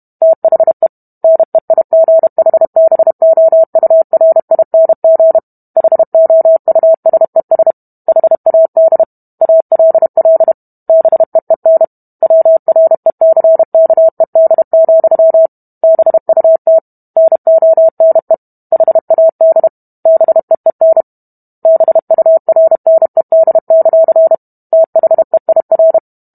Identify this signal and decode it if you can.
Morse code, heard as a keyed tone, about 32 wpm, 650 Hz. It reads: THE NEIGHBOURING HOUSES HAD ALL BEEN WRECKED, BUT NONE HAD BEEN BURNED; THEIR